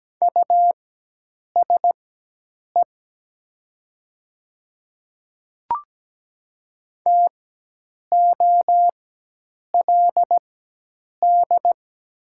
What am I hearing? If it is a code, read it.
Morse code, heard as a keyed tone, about 17 wpm, spaced as for 8 wpm, 700 Hz. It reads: USE TOLD
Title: 4x Character Spacing - Top 500 Words - 17wpm